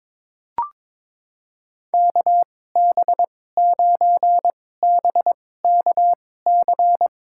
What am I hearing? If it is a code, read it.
Morse code, heard as a keyed tone, about 21 wpm, spaced as for 16 wpm, 700 Hz. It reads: KB9BKC
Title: USA Call Signs - 6 Characters - 2x Character Spacing - 22wpm